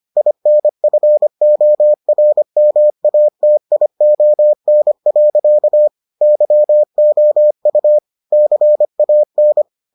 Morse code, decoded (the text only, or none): INFORMATION. YOU CAN